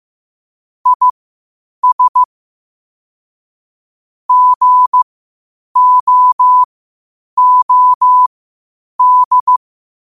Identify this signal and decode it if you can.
Morse code, heard as a keyed tone, about 15 wpm, spaced as for 8 wpm, 1000 Hz. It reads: IS GOOD
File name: qotdfiles / 8wpm